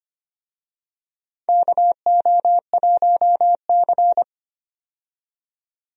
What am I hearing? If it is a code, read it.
Morse code, heard as a keyed tone, about 25 wpm, 700 Hz. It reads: KO1C
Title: USA Call Signs - 4 Characters - 25wpm